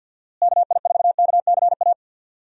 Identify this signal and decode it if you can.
Morse code, heard as a keyed tone, about 52 wpm, 700 Hz. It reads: KI4KCA
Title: USA Call Signs - 6 Characters - Repeated 3x - 50wpm